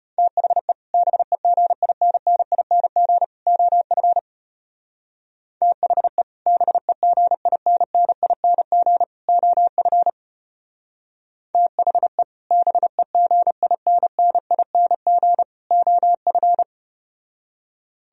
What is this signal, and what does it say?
Morse code, heard as a keyed tone, about 33 wpm, 700 Hz. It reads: THE BEGINNING OF THE BEGINNING OF THE BEGINNING OF